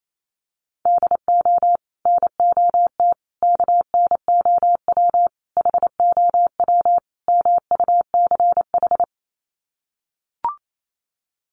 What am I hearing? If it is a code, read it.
Morse code, heard as a keyed tone, about 28 wpm, 700 Hz. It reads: DO NOT KNOW HOW MUCH
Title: Sets of 5 Words Encore 28wpm